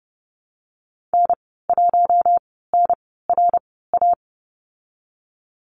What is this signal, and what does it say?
Morse code, heard as a keyed tone, about 32 wpm, spaced as for 17 wpm, 700 Hz. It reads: N1NRA
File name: Lesson 14g - Number 2 - Call Signs 30wpm